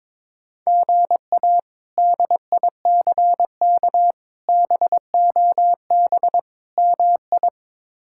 Morse code, heard as a keyed tone, about 22 wpm, 700 Hz. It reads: GA DICK BOB MI
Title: K1USN SST QSO Practice - 22wpm